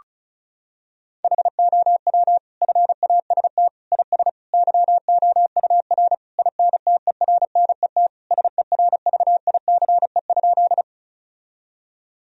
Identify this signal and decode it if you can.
Morse code, heard as a keyed tone, about 36 wpm, 700 Hz. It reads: HOW FAST IS YOUR INTERNET SERVICE?
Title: Sentences from Top 1000 Words - Review Part 4 - 35wpm